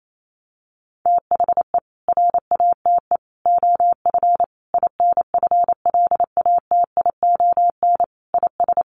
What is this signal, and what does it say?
Morse code, heard as a keyed tone, about 28 wpm, 700 Hz. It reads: THE RATE OF INFLATION IS
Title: Sentences from Top 1100 Words 28wpm